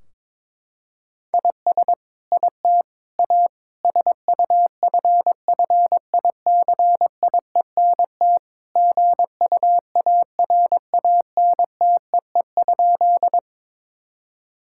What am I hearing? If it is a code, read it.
Morse code, heard as a keyed tone, about 22 wpm, 700 Hz. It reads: IS IT A SUFFICIENT GUARANTEE?
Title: Sentences from Top 1600 Words 22wpm